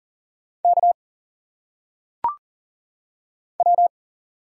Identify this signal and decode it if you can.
Morse code, heard as a keyed tone, about 40 wpm, 700 Hz. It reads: K W
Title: Single Letters - Phonetic Pronunciation 40wpm